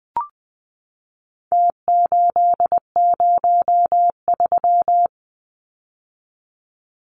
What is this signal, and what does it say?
Morse code, heard as a keyed tone, about 20 wpm, 700 Hz. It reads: T803
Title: Numbers - Contest Style - 4-Digit 20wpm